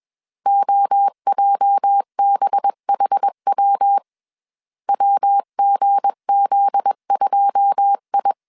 Morse code, heard as a keyed tone, about 21 wpm, 800 Hz. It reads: OJBHW WGZ2I
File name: CW / DET / 1 9 1500C S C 20wpm 2310